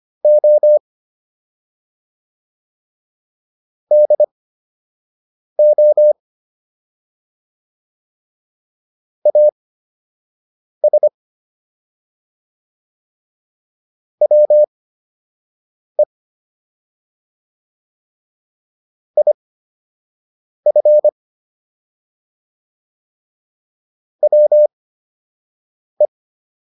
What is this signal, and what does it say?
Morse code, heard as a keyed tone, about 25 wpm, spaced as for 6 wpm, 600 Hz. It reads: O DO AS WE IF WE